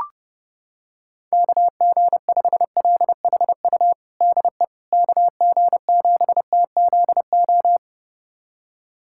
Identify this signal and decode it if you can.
Morse code, heard as a keyed tone, about 30 wpm, 700 Hz. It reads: KG5LHU DE KG7TZO
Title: USA Callsign DE Callsign - 6 Characters - 30wpm